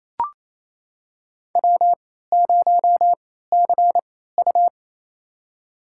Morse code, heard as a keyed tone, about 28 wpm, spaced as for 16 wpm, 700 Hz. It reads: W0CU
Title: USA Call Signs - 4 Characters - 3x Character Spacing - 28wpm